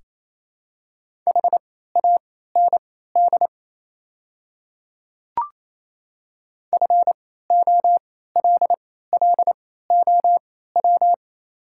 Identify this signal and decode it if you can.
Morse code, heard as a keyed tone, about 28 wpm, spaced as for 16 wpm, 700 Hz. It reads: HAND FOLLOW
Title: 3x Character Spacing - Top 200 Words - 28wpm